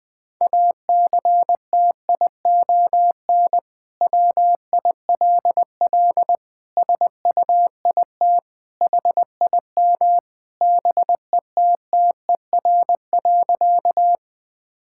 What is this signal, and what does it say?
Morse code, heard as a keyed tone, about 21 wpm, 700 Hz. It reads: ACTION WILL SUIT HIM BETTER.
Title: Sentences from Top 1600 Words 20wpm